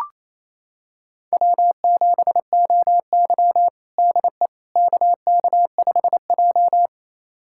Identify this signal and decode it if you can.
Morse code, heard as a keyed tone, about 28 wpm, 700 Hz. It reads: W7OY DE KK5J
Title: USA Callsign DE Callsign - 4 Characters - 28wpm